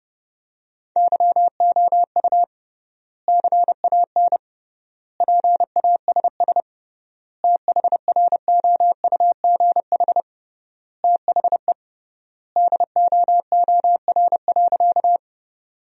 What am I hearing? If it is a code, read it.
Morse code, heard as a keyed tone, about 30 wpm, 700 Hz. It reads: YOU CAN PASS THROUGH THE DOOR.
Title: 3x Word-Spacing - Sentences from Top 400 Words - 30wpm